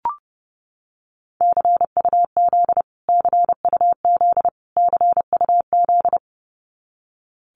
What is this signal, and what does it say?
Morse code, heard as a keyed tone, about 30 wpm, 700 Hz. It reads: CUZ CUZ CUZ
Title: Abbreviations Repeated 3x - 30wpm